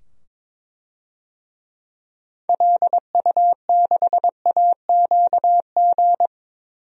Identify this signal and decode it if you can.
Morse code, heard as a keyed tone, about 22 wpm, 700 Hz. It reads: LU6AQG
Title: DXCC-100 Call Signs - 6 Characters - 22wpm